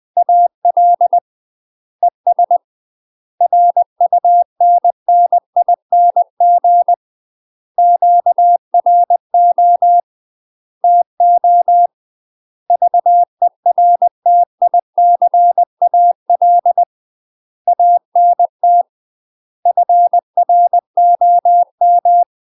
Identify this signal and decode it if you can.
Morse code, heard as a keyed tone, about 20 wpm, 700 Hz. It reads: AL ES RUNNING QRO TO VERTICAL ANT FROM